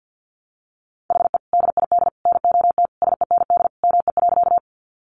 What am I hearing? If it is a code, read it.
Morse code, heard as a keyed tone, about 50 wpm, 700 Hz. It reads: HE DID NOT SEND ME.
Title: Sentences from Top 1500 Words 50wpm